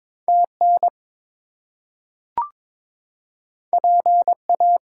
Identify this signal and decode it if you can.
Morse code, heard as a keyed tone, about 22 wpm, 700 Hz. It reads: TN PA
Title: US States and Canadian Provinces 22wpm